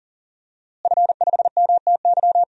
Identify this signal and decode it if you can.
Morse code, heard as a keyed tone, about 40 wpm, 700 Hz. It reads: F5MTY
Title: DXCC-100 Call Signs - 5 Characters - Repeated 3x - 40wpm